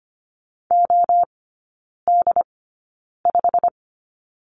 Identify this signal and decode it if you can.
Morse code, heard as a keyed tone, about 25 wpm, spaced as for 9 wpm, 690 Hz. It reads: OD5
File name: Lesson 15e - Number 5 - Three Characters 25wpm